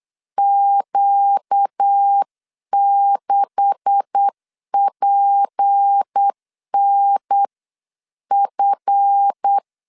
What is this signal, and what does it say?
Morse code, heard as a keyed tone, about 8 wpm, 800 Hz. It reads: Q6PNF
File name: CW / DET / 1 9 1500C S C 8wpm 5653